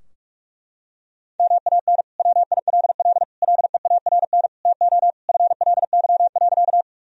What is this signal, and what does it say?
Morse code, heard as a keyed tone, about 46 wpm, 700 Hz. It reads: MAN WILL LEARN TO FLY.
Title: Sentences from Top 1000 Words - Review Part 2 - 45wpm